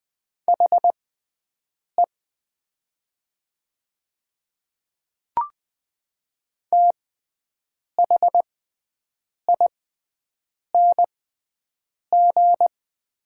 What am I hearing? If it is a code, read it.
Morse code, heard as a keyed tone, about 20 wpm, spaced as for 7 wpm, 700 Hz. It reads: HE THING
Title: 6x Character Spacing - Top 100 Words - 20wpm